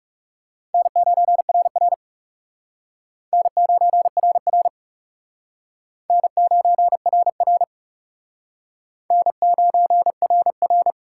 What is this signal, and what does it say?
Morse code, heard as a keyed tone, about 37 wpm, 700 Hz. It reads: N9RR N9RR N9RR N9RR
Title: Speed Racing - USA 1x2 Call Signs 30wpm